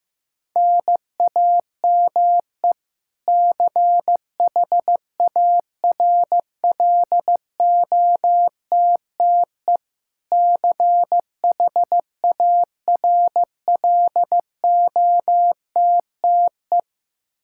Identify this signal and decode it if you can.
Morse code, heard as a keyed tone, about 15 wpm, 700 Hz. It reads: NAME CHARLOTTE CHARLOTTE
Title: Practice QSO - Exchange 1 - Beginning - 15wpm